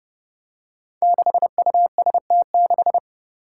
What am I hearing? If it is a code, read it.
Morse code, heard as a keyed tone, about 30 wpm, 700 Hz. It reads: 6UST6
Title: Five Challenging Letter-Number Combinations 30wpm